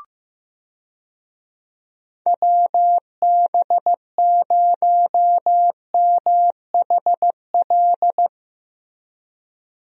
Morse code, heard as a keyed tone, about 16 wpm, 700 Hz. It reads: WB0MHL
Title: USA Call Signs - 6 Characters - Repeated 2x - 15wpm